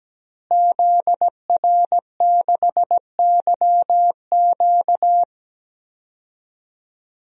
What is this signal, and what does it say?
Morse code, heard as a keyed tone, about 17 wpm, 700 Hz. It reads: ZR6YQ
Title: DXCC-100 Call Signs - 5 Characters - Repeated 3x - 17wpm